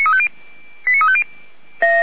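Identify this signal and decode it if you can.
Five-tone selective call (ZVEI), heard as quick runs of five tones, tones of about 70 ms. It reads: 89350
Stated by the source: ZVEI five-tone selective call 89230 received by SDR, from the Feuerwehr Orlamünde audio archive.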